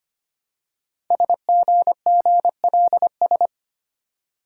Morse code, heard as a keyed tone, about 25 wpm, 700 Hz. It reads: SGGLS
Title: Code Groups - 5 Letters - 25wpm